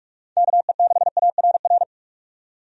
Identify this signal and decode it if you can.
Morse code, heard as a keyed tone, about 46 wpm, 700 Hz. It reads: KE6ARR
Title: USA Call Signs - 6 Characters - 45wpm